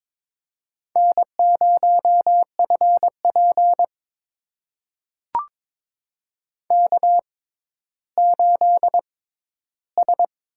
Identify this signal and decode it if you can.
Morse code, heard as a keyed tone, about 22 wpm, spaced as for 13 wpm, 700 Hz. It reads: N0FP K8S
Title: USA Call Signs - 4 Characters - 6x Character Spacing - 22wpm